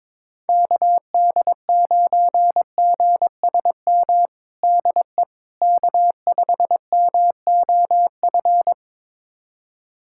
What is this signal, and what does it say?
Morse code, heard as a keyed tone, about 22 wpm, 700 Hz. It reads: KD9GSM DE K5MOF
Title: USA Callsign DE Callsign 22wpm